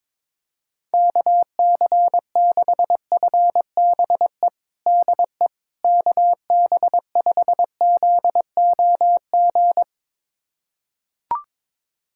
Morse code, heard as a keyed tone, about 21 wpm, 700 Hz. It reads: KC6FBE DE KB5ZOG
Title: USA Callsign DE Callsign - 6 Characters - 22wpm